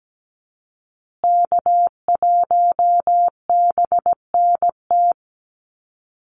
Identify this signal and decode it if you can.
Morse code, heard as a keyed tone, about 17 wpm, 700 Hz. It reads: K1BNT
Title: USA Call Signs Repeated 3x 17wpm